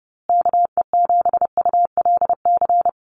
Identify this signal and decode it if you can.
Morse code, heard as a keyed tone, about 30 wpm, 700 Hz. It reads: KE7ULC
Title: USA Call Signs Repeated 2x 30wpm